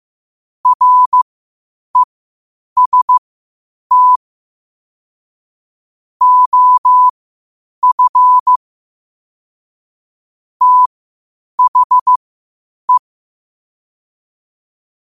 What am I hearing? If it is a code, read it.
Morse code, heard as a keyed tone, about 15 wpm, spaced as for 8 wpm, 1000 Hz. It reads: REST OF THE
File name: qotdfiles / 8wpm